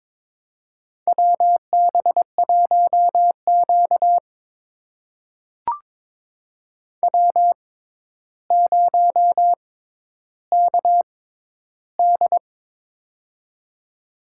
Morse code, heard as a keyed tone, about 22 wpm, spaced as for 11 wpm, 700 Hz. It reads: WB1Q W0KD
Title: USA Call Signs - 6x Character Spacing - 22wpm